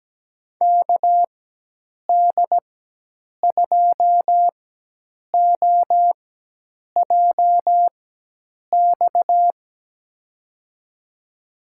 Morse code, heard as a keyed tone, about 17 wpm, spaced as for 8 wpm, 700 Hz. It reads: KD2OJX
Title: USA Call Signs - 6 Characters - 4x Character Spacing - 17wpm